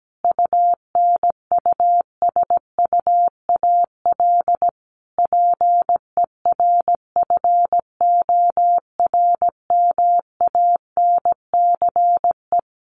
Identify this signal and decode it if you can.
Morse code, heard as a keyed tone, about 17 wpm, 700 Hz. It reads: UNUSUAL PERFORMANCE
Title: Sentences from Top 1400 Words 17wpm